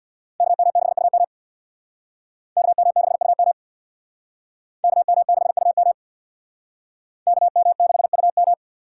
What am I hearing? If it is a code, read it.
Morse code, heard as a keyed tone, about 62 wpm, 690 Hz. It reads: KM6UM KM6UM KM6UM KM6UM
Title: Speed Racing - USA 2x2 Call Signs 50wpm